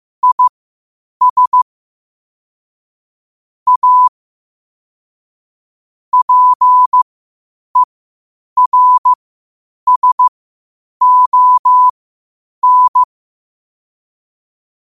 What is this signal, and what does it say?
Morse code, heard as a keyed tone, about 15 wpm, spaced as for 8 wpm, 1000 Hz. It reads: IS A PERSON
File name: qotdfiles / 8wpm